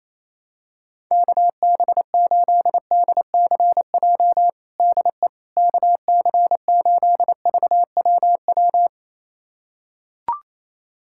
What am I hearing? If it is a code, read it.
Morse code, heard as a keyed tone, about 27 wpm, 700 Hz. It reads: KB8DCJ DE KC8VWW
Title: USA Callsign DE Callsign - 6 Characters - 28wpm